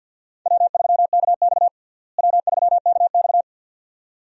Morse code, heard as a keyed tone, about 52 wpm, 700 Hz. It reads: W3KX W3KX
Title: USA Call Signs - 4 Characters - Repeated 2x - 50wpm